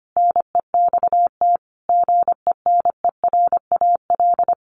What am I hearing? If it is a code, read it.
Morse code, heard as a keyed tone, about 25 wpm, 700 Hz. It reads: NEXT GENERAL
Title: Sentences from Top 1500 Words 25wpm